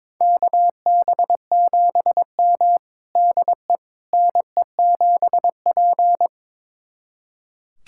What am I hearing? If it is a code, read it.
Morse code, heard as a keyed tone, about 22 wpm, 700 Hz. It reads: KB7M DE NE7P
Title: USA Callsign DE Callsign - 4 Characters - 22wpm